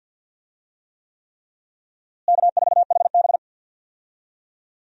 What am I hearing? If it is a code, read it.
Morse code, heard as a keyed tone, about 51 wpm, 700 Hz. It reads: K4HB
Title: USA Call Signs - 4 Characters - Repeated 3x - 50wpm